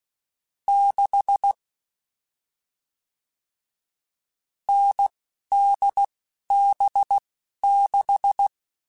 Morse code, heard as a keyed tone, about 16 wpm, spaced as for 11 wpm, 800 Hz. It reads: 6 NDB6
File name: Short Runs / Short Runs-mp3s / awj1ndb6gz7 @10